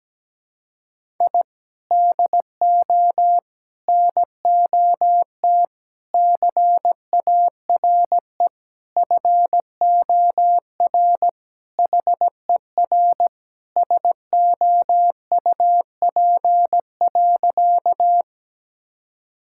Morse code, heard as a keyed tone, about 17 wpm, 700 Hz. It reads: I DO NOT CARE FOR HER SOUP.
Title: Sentences from Top 1000 Words - Review Part 6 - 17wpm